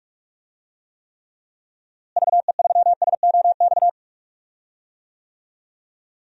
Morse code, heard as a keyed tone, about 41 wpm, 700 Hz. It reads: VE3SOX
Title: DXCC-100 Call Signs - 6 Characters - Repeated 3x - 45wpm